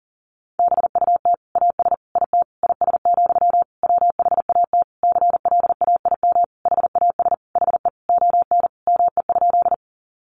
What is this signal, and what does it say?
Morse code, heard as a keyed tone, about 40 wpm, 700 Hz. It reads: BUT AS IT IS, WHAT CLAIM HAS HE ON ME?
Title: Sentences from Top 1500 Words 40wpm